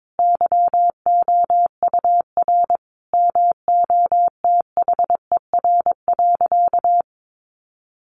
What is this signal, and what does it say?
Morse code, heard as a keyed tone, about 22 wpm, 700 Hz. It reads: YOUR MOTHER.
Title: Sentences from Top 1300 Words 22wpm